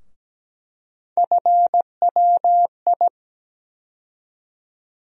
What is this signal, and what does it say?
Morse code, heard as a keyed tone, about 17 wpm, 700 Hz. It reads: FWI
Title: USA Call Sign Postfixes - 3 Letters 17wpm